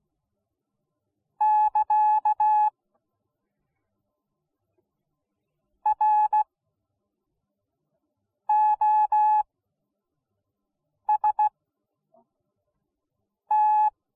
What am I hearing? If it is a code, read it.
Morse code, heard as a keyed tone, about 16 wpm, spaced as for 4 wpm, 850 Hz. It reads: <KA> ROST